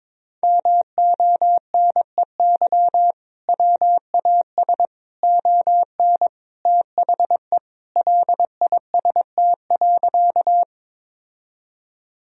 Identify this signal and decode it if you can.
Morse code, heard as a keyed tone, about 22 wpm, 700 Hz. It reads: MONEY WAS ON THE LIST.
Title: Sentences from Top 1000 Words - Review Part 2 - 22wpm